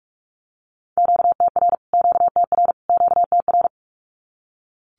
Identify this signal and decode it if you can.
Morse code, heard as a keyed tone, about 44 wpm, 700 Hz. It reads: QTR QTR QTR
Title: Q-Codes Repeated 3x 45wpm